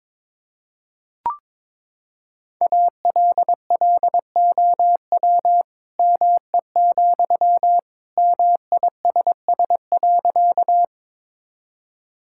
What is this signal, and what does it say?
Morse code, heard as a keyed tone, about 23 wpm, 700 Hz. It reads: ALLOW ME, MISS.
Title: Sentences from Top 1000 Words - Review Part 5 - 22wpm